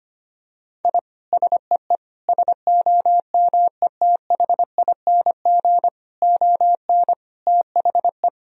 Morse code, heard as a keyed tone, about 25 wpm, 700 Hz. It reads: I SEE SOMETHING ON THE
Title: Sentences from Top 1000 Words - Review Part 2 - 25wpm